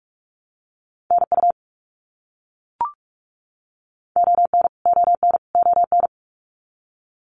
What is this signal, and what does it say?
Morse code, heard as a keyed tone, about 44 wpm, 700 Hz. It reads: NU ON ON ON
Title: Canadian Provinces and Territories - Repeated 3x - 45wpm